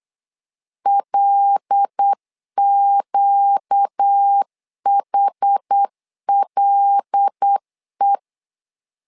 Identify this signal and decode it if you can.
Morse code, heard as a keyed tone, about 8 wpm, 800 Hz. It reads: LQHLE
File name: LEZIONI / 5 LBVC / 5 L 8wpm 2831